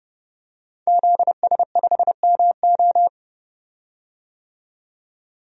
Morse code, heard as a keyed tone, about 30 wpm, 700 Hz. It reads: ZS5MO
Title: DXCC-100 Call Signs - 5 Characters - 30wpm